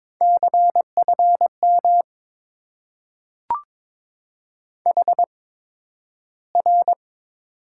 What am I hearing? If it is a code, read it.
Morse code, heard as a keyed tone, about 22 wpm, spaced as for 12 wpm, 700 Hz. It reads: CFM HR
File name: Top 90 QSO Elements - 8x Character Spacing - 22wpm